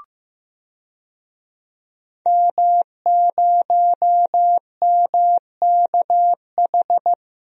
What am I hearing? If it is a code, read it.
Morse code, heard as a keyed tone, about 15 wpm, 700 Hz. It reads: M0MKH
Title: DXCC-100 Call Signs - 5 Characters - Repeated 2x - 15wpm